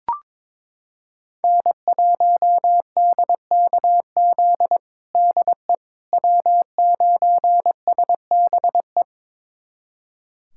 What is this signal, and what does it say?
Morse code, heard as a keyed tone, about 22 wpm, 700 Hz. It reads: N1DKZ DE W9SBE
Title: USA Callsign DE Callsign - 5 Characters - 22wpm